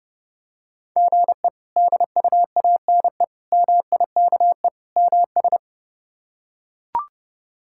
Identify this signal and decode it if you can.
Morse code, heard as a keyed tone, about 30 wpm, 700 Hz. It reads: GE DUANE MIKE MS
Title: K1USN SST QSO Practice - 30wpm